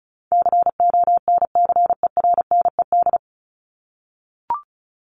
Morse code, heard as a keyed tone, about 34 wpm, 700 Hz. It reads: CONCERNED
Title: ED Suffix - Top 100 Words with Base Words 35wpm